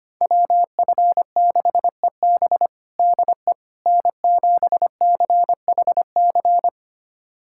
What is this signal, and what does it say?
Morse code, heard as a keyed tone, about 25 wpm, 700 Hz. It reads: WF6EB DE N7CHC
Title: USA Callsign DE Callsign - 5 Characters - 25wpm